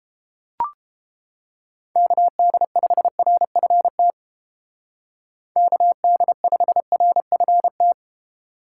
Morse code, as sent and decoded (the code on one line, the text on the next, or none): -.- -.. ..... .-. ..-. - / -.- -.. ..... .-. ..-. -
KD5RFT KD5RFT